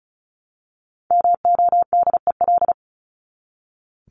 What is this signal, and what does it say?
Morse code, heard as a keyed tone, about 34 wpm, 700 Hz. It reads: MODEL
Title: Top 501-750 Words 35wpm